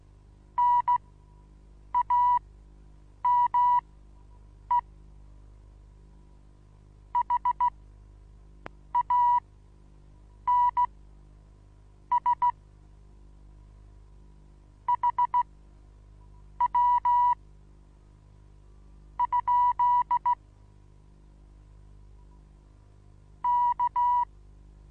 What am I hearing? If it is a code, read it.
Morse code, heard as a keyed tone, about 16 wpm, spaced as for 7 wpm, 970 Hz. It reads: NAME HANS HW ? K
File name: Onlinelehrgang / morsen / DJ4UF Morsen25